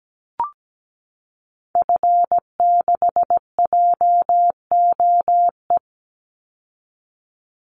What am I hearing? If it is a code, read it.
Morse code, heard as a keyed tone, about 17 wpm, 700 Hz. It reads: F6JOE
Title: Call Signs Repeated 3x - 17wpm